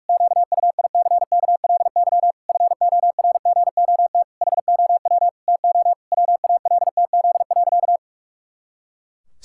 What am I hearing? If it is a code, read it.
Morse code, heard as a keyed tone, about 44 wpm, 700 Hz. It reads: QUICKLY FORGOT HOW TO WALTZ.
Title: Pangram Challenge v2 - 45wpm